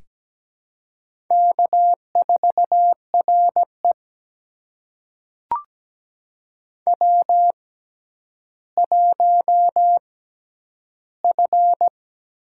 Morse code, heard as a keyed tone, about 17 wpm, spaced as for 10 wpm, 700 Hz. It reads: K4RE W1F
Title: USA Call Signs - 4 Characters - 6x Character Spacing - 17wpm